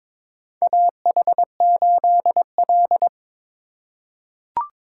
Morse code, heard as a keyed tone, about 21 wpm, 700 Hz. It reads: AH8L
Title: USA Call Signs - 4 Characters - 3x Character Spacing - 22wpm